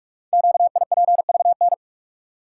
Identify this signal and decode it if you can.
Morse code, heard as a keyed tone, about 44 wpm, 700 Hz. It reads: QIPVN
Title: Code Groups - 5 Letter:Numbers - 45wpm